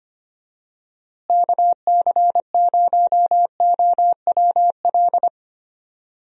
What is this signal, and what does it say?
Morse code, heard as a keyed tone, about 25 wpm, 700 Hz. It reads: KC0OWL
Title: USA Call Signs - 6 Characters - Repeated 3x - 25wpm